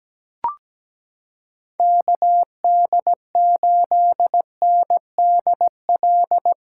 Morse code, heard as a keyed tone, about 17 wpm, 700 Hz. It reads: KD8NDL
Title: USA Callsign DE Callsign 17wpm